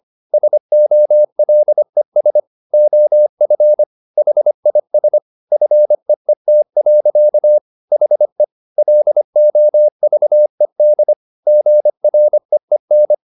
Morse code, heard as a keyed tone, about 25 wpm, 600 Hz. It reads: SOLES OF HIS FEET. HE LOVED GREEN
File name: the old man and the sea part 1 25wpm